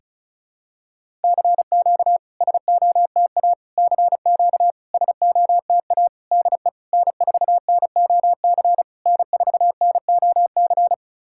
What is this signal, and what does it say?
Morse code, heard as a keyed tone, about 36 wpm, 700 Hz. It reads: CQ SOTA CQ SOTA DE N4NOC N4NOC
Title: SOTA QSO Practice 35wpm